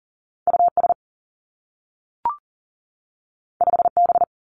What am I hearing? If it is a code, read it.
Morse code, heard as a keyed tone, about 40 wpm, 700 Hz. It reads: U S 5 B
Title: Two Challenging Letter-Number Combinations 40wpm